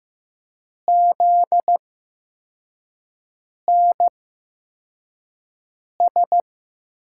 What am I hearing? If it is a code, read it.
Morse code, heard as a keyed tone, about 15 wpm, spaced as for 4 wpm, 700 Hz. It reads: ZNS
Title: USA Call Signs - 5 Characters - 8x Character Spacing - 15wpm